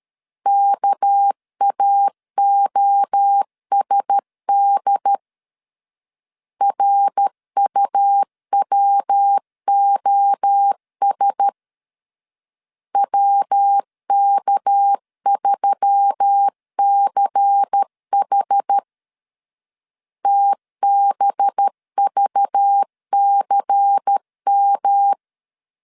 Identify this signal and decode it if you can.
Morse code, heard as a keyed tone, about 13 wpm, 800 Hz. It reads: KAOSD RUWOS WK3CH TBVCM
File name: LEZIONI / 6 PKX0 / 6 PK 12wpm 2006